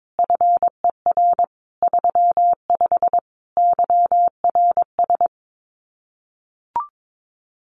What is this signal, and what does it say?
Morse code, heard as a keyed tone, about 22 wpm, 700 Hz. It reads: FER 35 YRS
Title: Common QSO Phrases Encore 22wpm